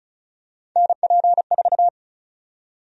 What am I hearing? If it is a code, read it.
Morse code, heard as a keyed tone, about 36 wpm, 700 Hz. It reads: NP4
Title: USA Call Sign Prefixes - 35wpm